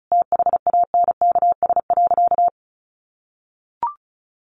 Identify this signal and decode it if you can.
Morse code, heard as a keyed tone, about 34 wpm, 700 Hz. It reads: THANKS.
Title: Sentences from Top 600 Words 35wpm